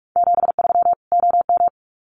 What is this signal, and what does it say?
Morse code, heard as a keyed tone, about 44 wpm, 700 Hz. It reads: 73 OM
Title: Common QSO Phrases 45wpm